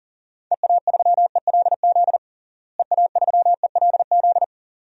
Morse code, heard as a keyed tone, about 40 wpm, 700 Hz. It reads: EA3ELZ EA3ELZ
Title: DXCC-100 Call Signs - 6 Characters - Repeated 2x - 40wpm